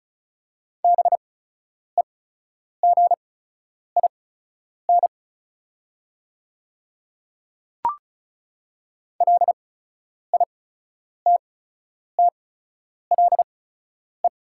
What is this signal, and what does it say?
Morse code, heard as a keyed tone, about 36 wpm, spaced as for 10 wpm, 700 Hz. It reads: BEGIN LITTLE
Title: 8x Character Spacing - Top 300 Words - 35wpm